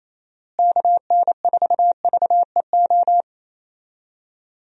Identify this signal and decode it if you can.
Morse code, heard as a keyed tone, about 28 wpm, 700 Hz. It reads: KN4VEO
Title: USA Call Signs - 6 Characters - Repeated 3x - 28wpm